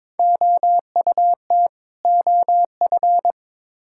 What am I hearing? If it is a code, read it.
Morse code, heard as a keyed tone, about 21 wpm, 700 Hz. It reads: OUT OF
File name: Build A Phrase - 2-Word Phrases - 22wpm